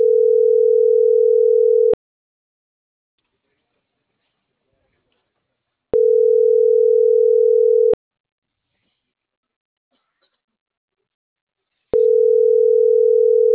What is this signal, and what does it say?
Ringback tone (North America), 440 + 480 Hz.